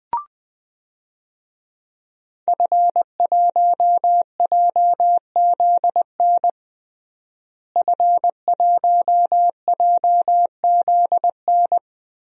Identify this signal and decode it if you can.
Morse code, heard as a keyed tone, about 21 wpm, 700 Hz. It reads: F1JZN F1JZN
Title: DXCC-100 Call Signs - 5 Characters - Repeated 2x - 20wpm